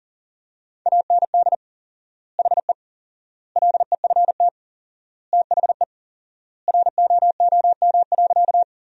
Morse code, heard as a keyed tone, about 40 wpm, 700 Hz. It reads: AND HE LEFT THE ROOM.
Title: 4x Word-Spacing - Sentences from Top 300 Words - 40wpm